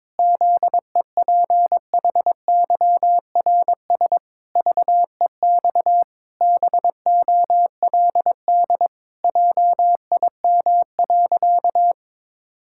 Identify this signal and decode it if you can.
Morse code, heard as a keyed tone, about 22 wpm, 700 Hz. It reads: ZEPHYRS VEX BOLD JIM.
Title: Pangram Challenge v2 - 22wpm